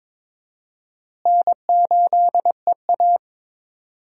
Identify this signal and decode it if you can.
Morse code, heard as a keyed tone, about 22 wpm, 700 Hz. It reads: N8EA
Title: USA Call Signs - 4 Characters - Repeated 3x - 22wpm